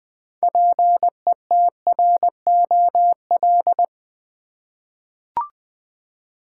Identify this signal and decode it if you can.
Morse code, heard as a keyed tone, about 20 wpm, 700 Hz. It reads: PETROL
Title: 6 Letter Words - First 3 Letters Given - 20wpm